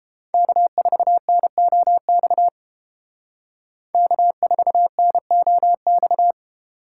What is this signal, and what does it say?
Morse code, heard as a keyed tone, about 30 wpm, 700 Hz. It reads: K4NOX K4NOX